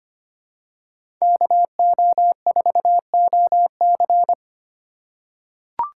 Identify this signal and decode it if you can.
Morse code, heard as a keyed tone, about 25 wpm, 700 Hz. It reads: KO4OC
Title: USA Call Signs - 5 Characters - 4x Character Spacing - 25wpm